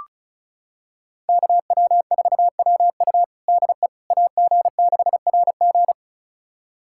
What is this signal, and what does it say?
Morse code, heard as a keyed tone, about 36 wpm, 700 Hz. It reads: KW4WU DE AG6RG
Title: USA Callsign DE Callsign - 5 Characters - 35wpm